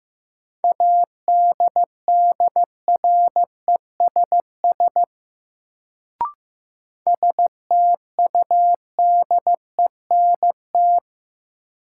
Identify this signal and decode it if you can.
Morse code, heard as a keyed tone, about 15 wpm, 700 Hz. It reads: ADDRESS STUDENT
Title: Sentences from Top 1000 Words - New Word Drill - 15wpm